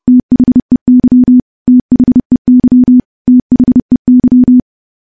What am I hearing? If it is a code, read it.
Morse code, heard as a keyed tone, about 32 wpm, 250 Hz. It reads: THEY THEY THEY
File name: morse / morse-training-30WPM